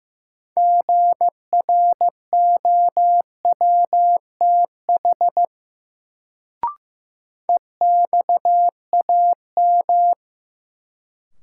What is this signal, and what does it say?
Morse code, heard as a keyed tone, about 15 wpm, 700 Hz. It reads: GROWTH EXAM
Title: Sentences from Top 1000 Words - Review Part 5 - New Word Drill - 15wpm